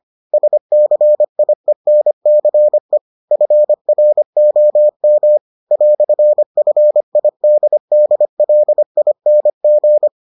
Morse code, heard as a keyed tone, about 25 wpm, 600 Hz. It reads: SCIENCE FROM "FIDDLING